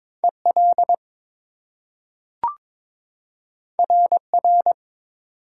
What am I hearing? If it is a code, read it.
Morse code, heard as a keyed tone, about 22 wpm, 700 Hz. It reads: EL RR
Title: Top 180 QSO Elements 22wpm